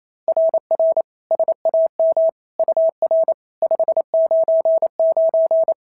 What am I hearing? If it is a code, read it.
Morse code, heard as a keyed tone, about 27 wpm, 690 Hz. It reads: RR SAM UR 599
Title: SOTA QSO Practice 28wpm